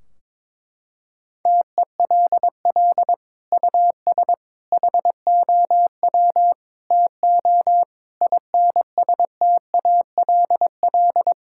Morse code, heard as a keyed tone, about 22 wpm, 700 Hz. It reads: TELL US HOW TO INSTALL
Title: Sentences from Top 1600 Words 22wpm